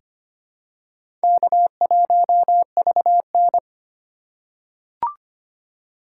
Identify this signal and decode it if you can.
Morse code, heard as a keyed tone, about 25 wpm, 700 Hz. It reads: K1VN
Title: USA Call Signs - 4 Characters - 3x Character Spacing - 25wpm